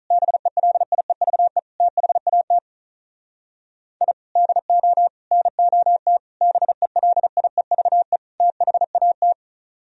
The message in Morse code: BELIEVE THAT I DO NOT BELIEVE THAT

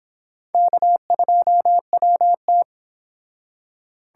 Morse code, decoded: K2WT

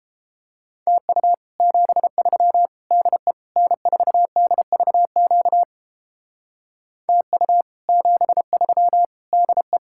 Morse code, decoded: TU 73 DE N4DVQ TU 73 DE

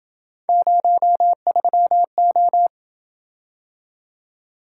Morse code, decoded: 03O

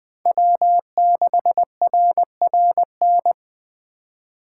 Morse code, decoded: W6RRN

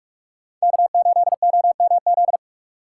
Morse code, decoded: K8OMZ

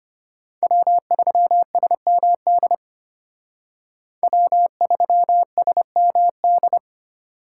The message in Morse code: W3SMD W3SMD

